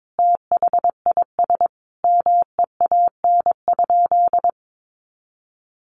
Morse code, heard as a keyed tone, about 22 wpm, 700 Hz. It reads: THIS MEAN?
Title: Sentences from Top 200 Words 22wpm